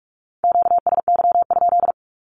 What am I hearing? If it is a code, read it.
Morse code, heard as a keyed tone, about 45 wpm, 700 Hz. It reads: QSY?
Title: Q-Codes Repeated 3x 45wpm